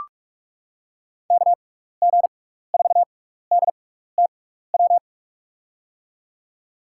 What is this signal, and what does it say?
Morse code, heard as a keyed tone, about 43 wpm, spaced as for 15 wpm, 700 Hz. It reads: KG4DTW
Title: USA Call Signs - 6 Characters - 6x Character Spacing - 45wpm